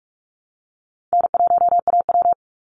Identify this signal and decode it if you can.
Morse code, heard as a keyed tone, about 46 wpm, 700 Hz. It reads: N1AW
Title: NA SOTA Chaser Callsigns 45wpm